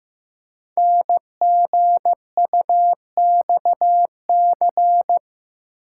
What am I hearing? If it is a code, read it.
Morse code, heard as a keyed tone, about 15 wpm, 700 Hz. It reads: NGUXC
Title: Code Groups - 5 Letter:Numbers - 15wpm